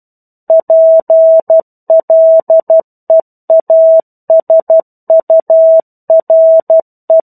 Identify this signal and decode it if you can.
Morse code, heard as a keyed tone, about 12 wpm, 650 Hz. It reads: PLEASURE
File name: 01-Chapter-01 12 WPM